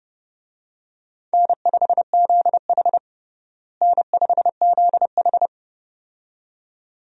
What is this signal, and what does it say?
Morse code, heard as a keyed tone, about 31 wpm, 700 Hz. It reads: N5ZH N5ZH
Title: USA Call Signs - 4 Characters - Repeated 2x - 30wpm